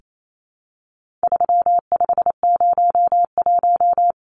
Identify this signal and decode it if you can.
Morse code, heard as a keyed tone, about 28 wpm, 700 Hz. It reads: 3501
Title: Numbers - 4-Digit 28wpm